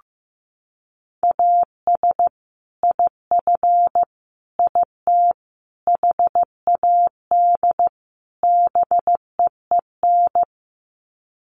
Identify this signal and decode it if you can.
Morse code, heard as a keyed tone, about 15 wpm, 700 Hz. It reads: AS IF IT HAD BEEN
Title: Sets of 5 Words Encore2 15wpm